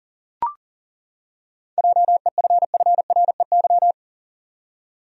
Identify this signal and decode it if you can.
Morse code, heard as a keyed tone, about 40 wpm, 700 Hz. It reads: JEFFREY
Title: USA Common Names - 40wpm